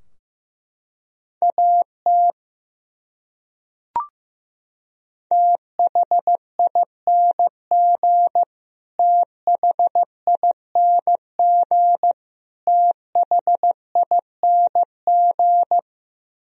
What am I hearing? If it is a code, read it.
Morse code, heard as a keyed tone, about 15 wpm, 700 Hz. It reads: AT THING THING THING
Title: Top 100 Words Repeated 15wpm